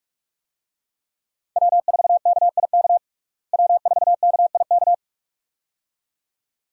Morse code, heard as a keyed tone, about 41 wpm, 700 Hz. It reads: W4KIK W4KIK
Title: USA Call Signs - 5 Characters - Repeated 2x - 45wpm